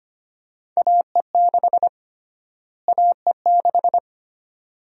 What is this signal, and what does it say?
Morse code, heard as a keyed tone, about 25 wpm, 700 Hz. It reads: AE6 AE6
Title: USA Call Sign Prefixes - Repeated 2x - 25wpm